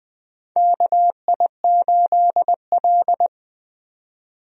Morse code, heard as a keyed tone, about 20 wpm, 700 Hz. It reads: KI8L
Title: USA Call Signs - 4 Characters - 20wpm